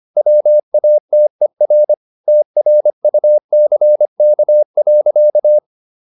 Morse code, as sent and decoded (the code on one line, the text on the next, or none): .-- .- - . .-. / - .-. ..- -.-. -.- .-.-.-
WATER TRUCK.